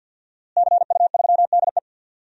Morse code, heard as a keyed tone, about 46 wpm, 700 Hz. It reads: CU3DE
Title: DXCC-100 Call Signs - 5 Characters - Repeated 3x - 50wpm